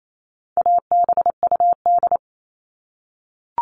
Morse code, heard as a keyed tone, about 28 wpm, 700 Hz. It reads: ABUD
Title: All Cut Numbers - 4-Digit 28wpm